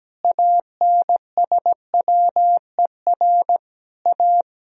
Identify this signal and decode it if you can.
Morse code, heard as a keyed tone, about 17 wpm, 700 Hz. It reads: ANSWER A
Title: Sentences from Top 1000 Words - Review Part 4 - 17wpm